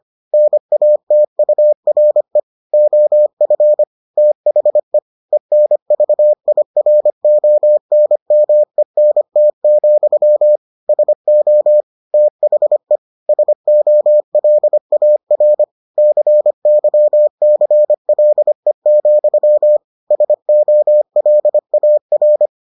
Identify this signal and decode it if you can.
Morse code, heard as a keyed tone, about 25 wpm, 600 Hz. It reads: NATURE OF THE ENVIRONMENT, SO THE SOLAR CYCLE, SOLAR